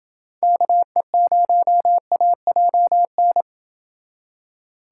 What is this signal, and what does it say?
Morse code, heard as a keyed tone, about 29 wpm, 700 Hz. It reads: KE0AJN